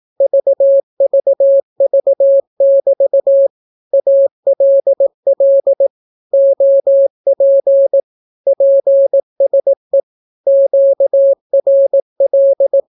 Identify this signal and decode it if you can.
Morse code, heard as a keyed tone, about 18 wpm, 550 Hz. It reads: VVV<BT> ALL OP PSE QRL
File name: naukapodstaw / skroty / skroty